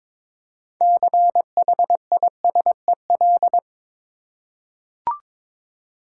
Morse code, heard as a keyed tone, about 22 wpm, 700 Hz. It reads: CHISEL
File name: Difficult Words with Hints - 22wpm